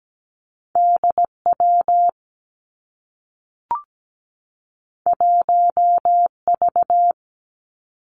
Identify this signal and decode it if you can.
Morse code, heard as a keyed tone, about 17 wpm, 700 Hz. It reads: DW 1V
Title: Two Challenging Letter-Number Combinations 17wpm